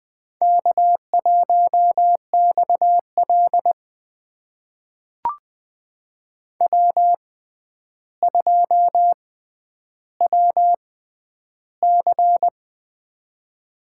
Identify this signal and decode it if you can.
Morse code, heard as a keyed tone, about 20 wpm, spaced as for 10 wpm, 700 Hz. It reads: K1XL W2WC
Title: USA Call Signs - 4 Characters - 6x Character Spacing - 20wpm